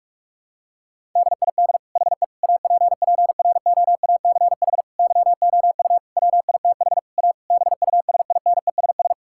Morse code, heard as a keyed tone, about 42 wpm, 700 Hz. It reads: DID HE APPROACH YOU WITH A BUSINESS
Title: Sentences from Top 1000 Words - Review Part 6 - 45wpm